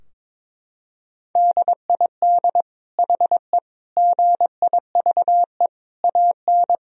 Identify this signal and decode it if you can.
Morse code, heard as a keyed tone, about 22 wpm, 700 Hz. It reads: DID HE GIVE AN
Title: Sentences from Top 1000 Words - Review Part 3 - 22wpm